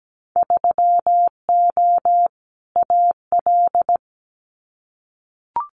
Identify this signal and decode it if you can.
Morse code, heard as a keyed tone, about 17 wpm, 690 Hz. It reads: 3O AL
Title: Winter Field Day Exchanges 17wpm